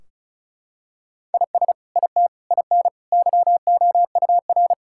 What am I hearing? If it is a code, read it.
Morse code, heard as a keyed tone, about 34 wpm, 700 Hz. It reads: IS IT IN YOUR
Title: Sentences from Top 1600 Words 35wpm